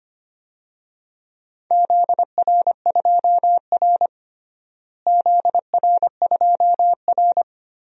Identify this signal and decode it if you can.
Morse code, heard as a keyed tone, about 25 wpm, 700 Hz. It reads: ZR2R ZR2R
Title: DXCC-100 Call Signs - 4 Characters - Repeated 3x - 25wpm